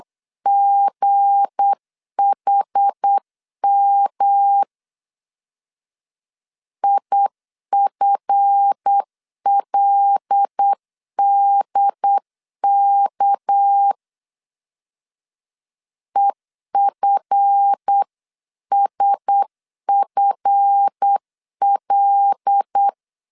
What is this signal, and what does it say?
Morse code, heard as a keyed tone, about 8 wpm, 800 Hz. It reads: GHM IFLDK EFSFL
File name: LEZIONI / 7 GF9Z / 7 GF 8wpm 3108